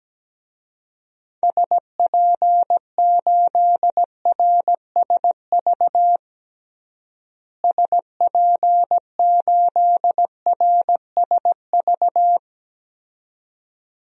Morse code, heard as a keyed tone, about 17 wpm, 700 Hz. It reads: SP8RSV SP8RSV
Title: DXCC-100 Call Signs - 6 Characters - Repeated 2x - 17wpm